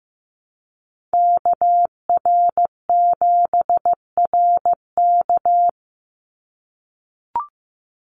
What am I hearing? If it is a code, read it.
Morse code, heard as a keyed tone, about 15 wpm, 700 Hz. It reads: KR7RK
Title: NA SOTA Activator Callsigns 15wpm